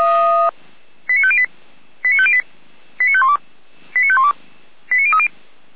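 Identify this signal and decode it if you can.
Five-tone selective call (ZVEI), heard as quick runs of five tones, tones of about 70 ms. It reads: 89508 89508 89512 89512 89030
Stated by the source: ZVEI five-tone call received on SDR.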